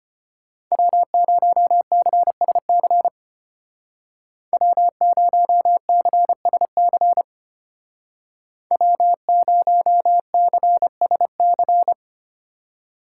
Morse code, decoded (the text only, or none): W0CSC W0CSC W0CSC